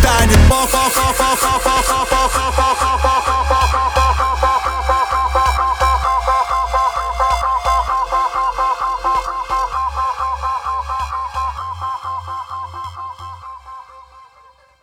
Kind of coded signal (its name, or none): none